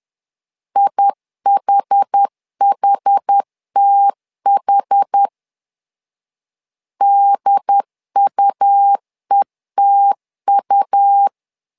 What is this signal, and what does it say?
Morse code, heard as a keyed tone, about 10 wpm, 800 Hz. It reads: IHHTH DUETU
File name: LEZIONI / 2 UHDN / 2 UHD 10wpm 1826